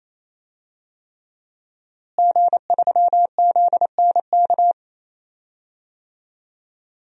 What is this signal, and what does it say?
Morse code, heard as a keyed tone, about 28 wpm, 700 Hz. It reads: G3ZNK
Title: DXCC-100 Call Signs - 5 Characters - Repeated 3x - 28wpm